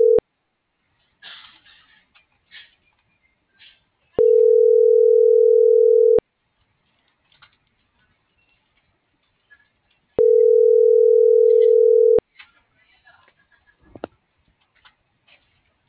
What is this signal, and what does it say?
Ringback tone (North America), 440 + 480 Hz.
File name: CENTRAL GOMSA / out garcia